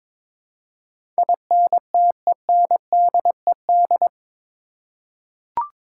Morse code, heard as a keyed tone, about 22 wpm, 700 Hz. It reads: INTENDED
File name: practice / 22wpm / Eight Letter Words - 22wpm